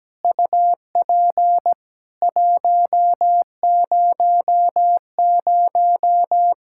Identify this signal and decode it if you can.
Morse code, heard as a keyed tone, about 17 wpm, 700 Hz. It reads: UP 100
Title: Practice QSO - Exchange 2 - Info - 17wpm